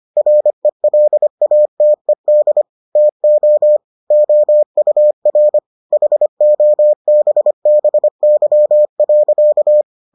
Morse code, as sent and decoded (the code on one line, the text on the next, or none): .-. .-.. .- - . -.. / - --- / --- ..- .-. / .... --- -... -... -.-- .-.-.-
RELATED TO OUR HOBBY.